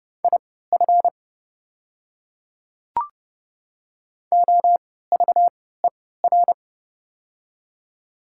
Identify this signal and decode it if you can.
Morse code, heard as a keyed tone, about 30 wpm, spaced as for 17 wpm, 700 Hz. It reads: IF OVER
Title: 3x Character Spacing - Top 200 Words - 30wpm